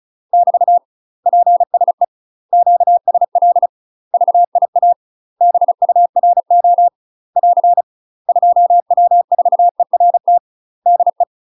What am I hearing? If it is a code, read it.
Morse code, heard as a keyed tone, about 35 wpm, 700 Hz. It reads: <BT> PSE QSL VIA BURO <AR> 2W4ERT DE
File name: UK MASTER QSOs TRAINING / 35WPM / UK MASTER QSOs 0001